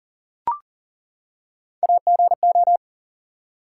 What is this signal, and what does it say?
Morse code, heard as a keyed tone, about 41 wpm, 700 Hz. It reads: AGO
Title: Three Letter Words - 40wpm